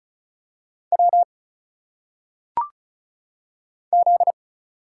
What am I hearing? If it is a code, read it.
Morse code, heard as a keyed tone, about 36 wpm, 700 Hz. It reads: W Z
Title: Single Letters - Phonetic Pronunciation 35wpm